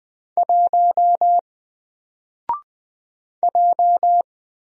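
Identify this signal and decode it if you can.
Morse code, heard as a keyed tone, about 20 wpm, 700 Hz. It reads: 1 J
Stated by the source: Single letter and number Morse drill at 20 wpm, from N4LCD.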